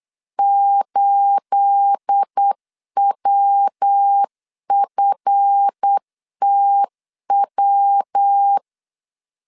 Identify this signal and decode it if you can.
Morse code, heard as a keyed tone, about 8 wpm, 800 Hz. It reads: 8WFTW
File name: CW / DET / 1 9 1500C S A 8wpm 5637